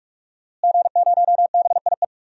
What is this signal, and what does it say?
Morse code, heard as a keyed tone, about 42 wpm, 700 Hz. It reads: G0BIE
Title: DXCC-100 Call Signs - 5 Characters - Repeated 3x - 45wpm